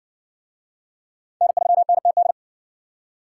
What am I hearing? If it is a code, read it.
Morse code, heard as a keyed tone, about 60 wpm, 700 Hz. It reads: N3NTD